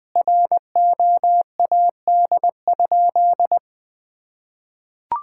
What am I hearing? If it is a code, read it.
Morse code, heard as a keyed tone, about 21 wpm, 700 Hz. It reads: ROAD?
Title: Sentences from Top 1000 Words - Review Part 5 - 20wpm